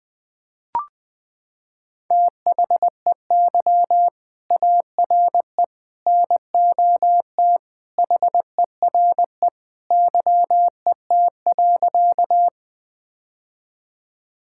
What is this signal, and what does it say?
Morse code, heard as a keyed tone, about 21 wpm, 700 Hz. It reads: THEY ARE NOT HERE YET.
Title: Sentences with 2-Word Phrases - 20wpm